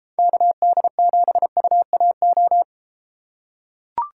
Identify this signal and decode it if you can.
Morse code, heard as a keyed tone, about 31 wpm, 700 Hz. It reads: KD7UAO